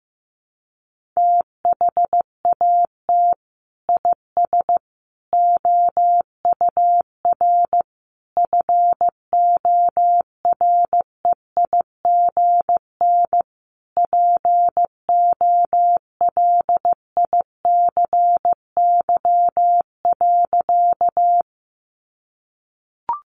Morse code, heard as a keyed tone, about 15 wpm, 700 Hz. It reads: THAT IS OUR FOREIGN POLICY.
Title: Sentences from Top 1400 Words 15wpm